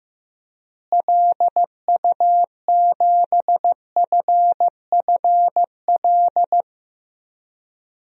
Morse code, heard as a keyed tone, about 15 wpm, 700 Hz. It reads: LU7FFL
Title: DXCC-100 Call Signs - 6 Characters - Repeated 3x - 15wpm